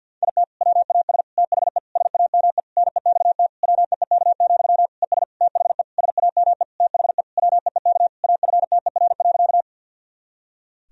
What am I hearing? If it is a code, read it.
Morse code, heard as a keyed tone, about 52 wpm, 700 Hz. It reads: IT WAS THE SAME NEXT WEEK, ES THE SAME THE WEEK AFTER.
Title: Sentences from Top 500 Words and Abbreviations 50wpm